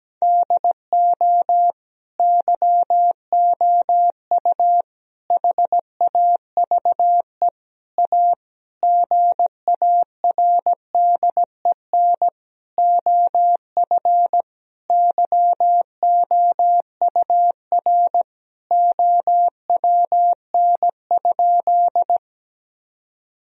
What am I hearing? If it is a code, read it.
Morse code, heard as a keyed tone, about 17 wpm, 700 Hz. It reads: DO YOU HAVE A GARDEN OF YOUR OWN?
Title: Sentences from Top 1000 Words - Review Part 4 - 17wpm